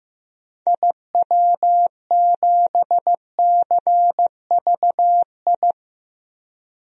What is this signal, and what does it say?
Morse code, heard as a keyed tone, about 15 wpm, 700 Hz. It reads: IW7CVI
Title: DXCC-100 Call Signs - 6 Characters - Repeated 3x - 15wpm